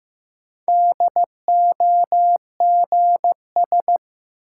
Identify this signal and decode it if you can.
Morse code, heard as a keyed tone, about 15 wpm, 700 Hz. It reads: DOGS
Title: Sentences from Top 1600 Words - New Word Drill Alt - 15wpm